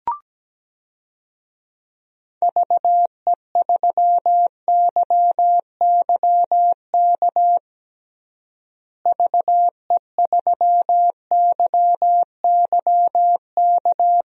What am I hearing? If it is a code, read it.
Morse code, heard as a keyed tone, about 17 wpm, 700 Hz. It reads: VE3YYK VE3YYK
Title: DXCC-100 Call Signs - 6 Characters - Repeated 3x - 17wpm